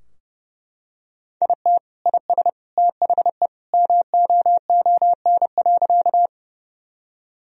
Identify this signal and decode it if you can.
Morse code, heard as a keyed tone, about 30 wpm, 700 Hz. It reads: IT IS THE MOON.
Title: Sentences from Top 1000 Words - Review Part 3 - 30wpm